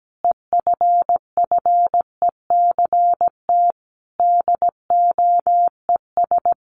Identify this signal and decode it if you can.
Morse code, heard as a keyed tone, about 17 wpm, 700 Hz. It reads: EFFECT DOES
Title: Sentences from Top 1000 Words 17wpm